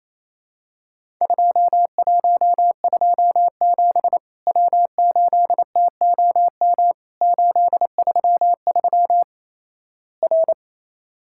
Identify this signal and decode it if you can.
Morse code, heard as a keyed tone, about 28 wpm, 700 Hz. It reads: 2127 W8TOM 833 R
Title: Worked All Europe DX Contest - QTC Exchange - 28wpm